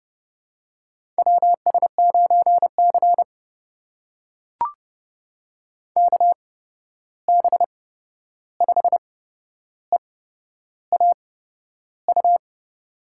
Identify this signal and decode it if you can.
Morse code, heard as a keyed tone, about 30 wpm, spaced as for 11 wpm, 700 Hz. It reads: WS9C KB5EAU
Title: USA Call Signs - 8x Character Spacing - 30wpm